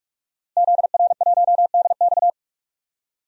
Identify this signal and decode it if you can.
Morse code, heard as a keyed tone, about 46 wpm, 700 Hz. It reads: ZR1DX
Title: DXCC-100 Call Signs - 5 Characters - Repeated 3x - 45wpm